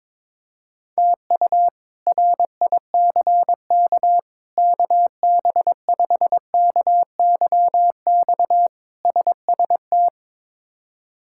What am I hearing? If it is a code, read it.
Morse code, heard as a keyed tone, about 22 wpm, 700 Hz. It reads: TU RICK KB5KYX SST
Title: K1USN SST QSO Practice - 22wpm